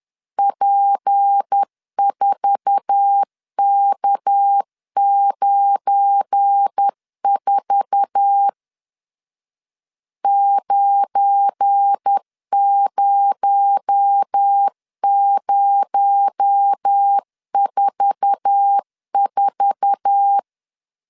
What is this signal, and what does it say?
Morse code, heard as a keyed tone, about 10 wpm, 800 Hz. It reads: P4K94 90044
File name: LEZIONI / 8 YJ45 / 8 YJ4 10wpm 2633